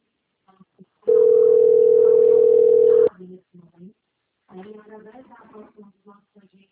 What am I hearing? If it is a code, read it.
Ringback tone (North America), 440 + 480 Hz, with speech.
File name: rg-1600-unknown- - - .20187